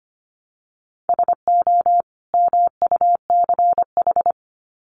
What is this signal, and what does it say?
Morse code, heard as a keyed tone, about 25 wpm, 700 Hz. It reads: SO MUCH
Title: Sets of 2 Words 25wpm